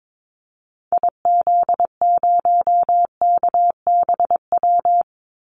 Morse code, heard as a keyed tone, about 22 wpm, 700 Hz. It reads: IZ0KBW
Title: Call Signs Repeated 2x - 22wpm